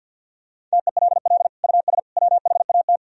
Morse code, heard as a keyed tone, about 50 wpm, 700 Hz. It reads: TELL US WHAT
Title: Sentences from Top 1000 Words - Review Part 4 - 50wpm